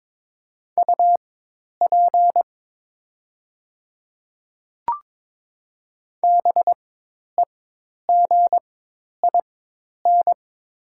Morse code, heard as a keyed tone, about 22 wpm, spaced as for 10 wpm, 700 Hz. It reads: UP BEGIN
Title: 4x Character Spacing - Top 300 Words - 22wpm